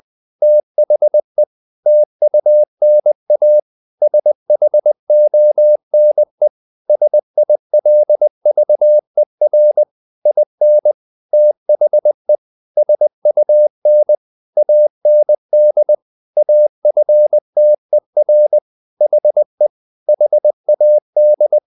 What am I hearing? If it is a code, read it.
Morse code, heard as a keyed tone, about 20 wpm, 600 Hz. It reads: THE TUNA SHONE SILVER IN THE SUN AND AFTER HE HAD
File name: the old man and the sea / the old man and the sea part 1 20wpm